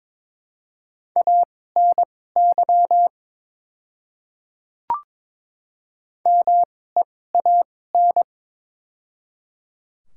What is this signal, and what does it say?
Morse code, heard as a keyed tone, about 22 wpm, spaced as for 16 wpm, 700 Hz. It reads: ANY MEAN
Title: 2x Character Spacing - Top 300 Words - 22wpm